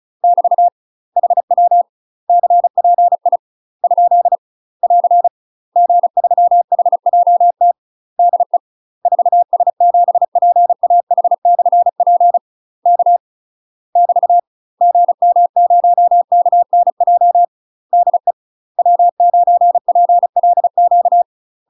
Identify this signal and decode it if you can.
Morse code, heard as a keyed tone, about 35 wpm, 700 Hz. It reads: <BT> HW CPI ? <AR> G3HJT DE 4S7PAH/P K <BT> GM0KNJ DE W9PLQ
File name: UK MASTER QSOs TRAINING / 35WPM / UK MASTER QSOs 0037